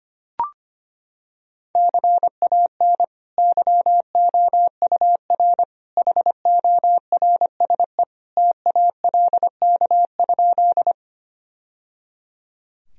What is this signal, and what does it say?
Morse code, heard as a keyed tone, about 25 wpm, 700 Hz. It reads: CAN YOUR HORSE TALK?
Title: Sentences from Top 1000 Words - Review Part 1 - 25wpm